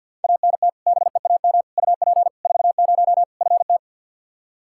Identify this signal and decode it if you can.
Morse code, heard as a keyed tone, about 48 wpm, 700 Hz. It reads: ANT BEAM UP 40 FT
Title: Practice QSO - Exchange 2 - Info - 50wpm